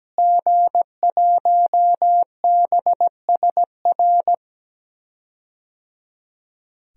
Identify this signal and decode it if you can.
Morse code, heard as a keyed tone, about 17 wpm, 700 Hz. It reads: G1BSR
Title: DXCC-100 Call Signs - 5 Characters - Repeated 3x - 17wpm